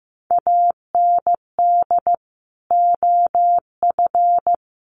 Morse code, heard as a keyed tone, about 15 wpm, 700 Hz. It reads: AND OF
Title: Sentences from Top 1500 Words 15wpm